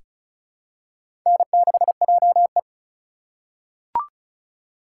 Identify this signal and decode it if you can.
Morse code, heard as a keyed tone, about 34 wpm, 700 Hz. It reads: N6JE
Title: USA Call Signs - 4 Characters - 8x Character Spacing - 35wpm